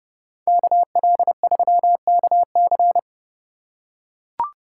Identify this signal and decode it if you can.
Morse code, heard as a keyed tone, about 29 wpm, 700 Hz. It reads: KL3KC